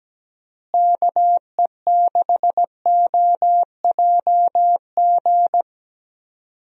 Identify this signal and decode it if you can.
Morse code, heard as a keyed tone, about 17 wpm, 700 Hz. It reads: KE6OJG
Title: USA Call Signs - 6 Characters - Repeated 3x - 17wpm